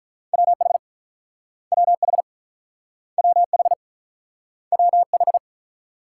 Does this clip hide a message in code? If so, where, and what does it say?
Yes: Morse code from 0.3 to 5.4 s, heard as a keyed tone, about 44 wpm, 700 Hz. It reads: WH WH WH WH